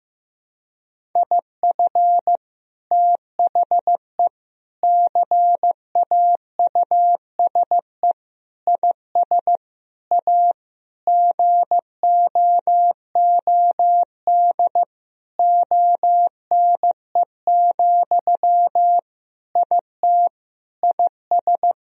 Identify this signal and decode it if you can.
Morse code, heard as a keyed tone, about 15 wpm, 700 Hz. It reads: IF THE CAUSE IS A GOOD ONE, IT IS
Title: Sentences from Top 1000 Words - Review Part 1 - 15wpm